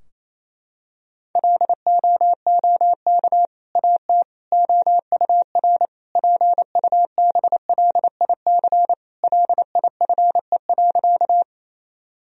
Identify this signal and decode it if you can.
Morse code, heard as a keyed tone, about 27 wpm, 700 Hz. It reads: LOOK AT OUR PUBLIC LIFE.
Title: Sentences from Top 1000 Words - Review Part 3 - 28wpm